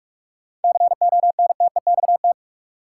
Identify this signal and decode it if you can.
Morse code, heard as a keyed tone, about 46 wpm, 700 Hz. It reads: CONTEXT
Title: Sentences from Top 1000 Words - Review Part 7 - New Word Drill - 45wpm